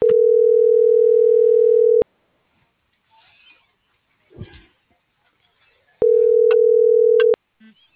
Ringback tone (North America), 440 + 480 Hz.